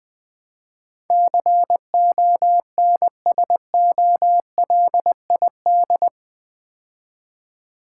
Morse code, heard as a keyed tone, about 20 wpm, 700 Hz. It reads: CONSOLID
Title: Build A Word - 4 Letters At A Time - 11-Letter Words - 20wpm